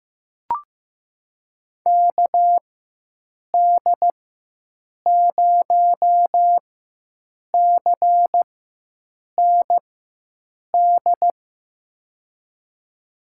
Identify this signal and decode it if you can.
Morse code, heard as a keyed tone, about 15 wpm, spaced as for 7 wpm, 700 Hz. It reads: KD0CND
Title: USA Call Signs - 4x Character Spacing - 15wpm